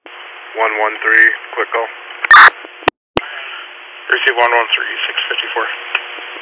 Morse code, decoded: none

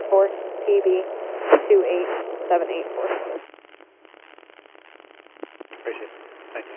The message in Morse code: none

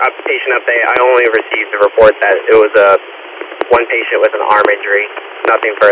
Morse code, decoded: none

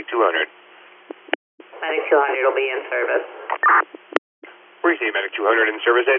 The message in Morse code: none

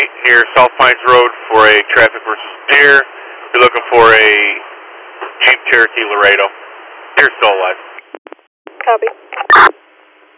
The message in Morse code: none

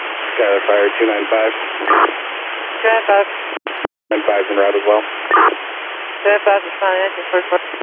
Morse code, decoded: none

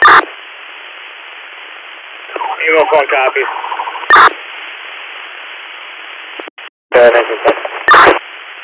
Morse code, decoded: none

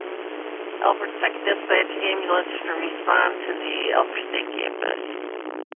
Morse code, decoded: none

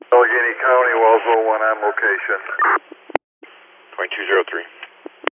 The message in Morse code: none

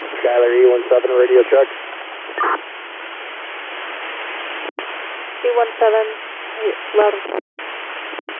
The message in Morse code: none

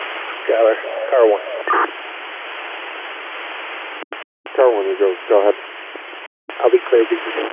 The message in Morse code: none

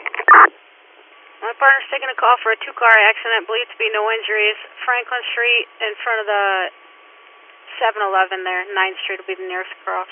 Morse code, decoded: none

